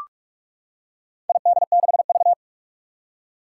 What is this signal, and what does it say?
Morse code, heard as a keyed tone, about 42 wpm, 700 Hz. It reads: ID6V
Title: Four Challenging Letter-Number Combinations 45wpm